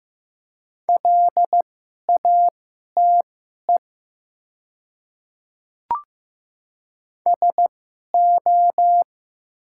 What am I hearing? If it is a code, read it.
Morse code, heard as a keyed tone, about 15 wpm, spaced as for 11 wpm, 700 Hz. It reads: LATE SO
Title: 2x Character Spacing - Top 300 Words - 15wpm